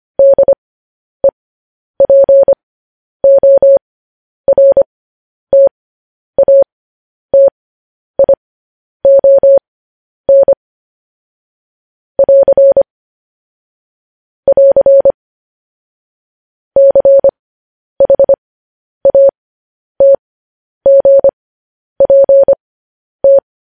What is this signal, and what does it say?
Morse code, heard as a keyed tone, about 25 wpm, spaced as for 10 wpm, 550 Hz. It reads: DEPORTATION <AR> <AR> CHATGPT
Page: news headlines in morse code at 10 wpm